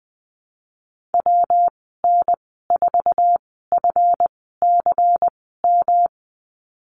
Morse code, heard as a keyed tone, about 20 wpm, spaced as for 15 wpm, 700 Hz. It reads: WN4FCM
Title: 147 Lesson 24g - Letter y - Call Signs 20wpm